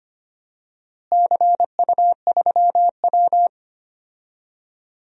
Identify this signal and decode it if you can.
Morse code, heard as a keyed tone, about 25 wpm, 700 Hz. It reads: CU3W
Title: DXCC-100 Call Signs - 4 Characters - Repeated 3x - 25wpm